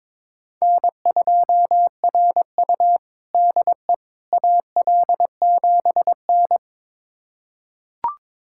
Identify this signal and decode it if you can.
Morse code, heard as a keyed tone, about 21 wpm, 700 Hz. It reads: N2RU DE AL7N
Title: USA Callsign DE Callsign - 4 Characters - 22wpm